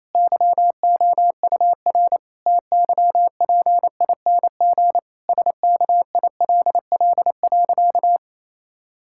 Morse code, heard as a keyed tone, about 28 wpm, 700 Hz. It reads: YOUR TYPING SKILL.
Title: Pangram Challenge v2 - 28wpm